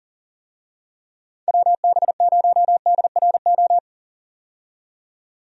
Morse code, heard as a keyed tone, about 39 wpm, 700 Hz. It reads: WB0DRO
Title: USA Call Signs - 6 Characters - 40wpm